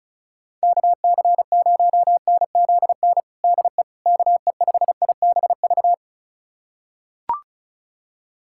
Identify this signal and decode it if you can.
Morse code, heard as a keyed tone, about 34 wpm, 690 Hz. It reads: KC0NZN DE KE5IBV
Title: USA Callsign DE Callsign 35wpm